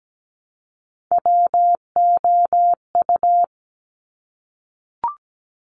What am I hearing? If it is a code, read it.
Morse code, heard as a keyed tone, about 17 wpm, 700 Hz. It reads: WOU
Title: Top Three Letter Combinations in Words 17wpm